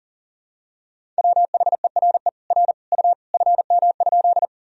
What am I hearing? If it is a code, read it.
Morse code, heard as a keyed tone, about 37 wpm, 700 Hz. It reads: WHERE R U FM?
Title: Sentences from Top 200 Words and Abbreviations 40wpm